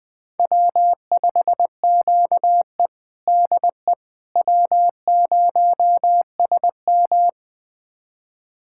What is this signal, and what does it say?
Morse code, heard as a keyed tone, about 20 wpm, 700 Hz. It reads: W5QE DE W0SM
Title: USA Callsign DE Callsign - 4 Characters - 20wpm